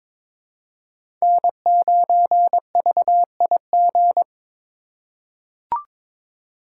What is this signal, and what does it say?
Morse code, heard as a keyed tone, about 21 wpm, 700 Hz. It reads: N9VIG
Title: USA Call Signs - 3x Character Spacing - 22wpm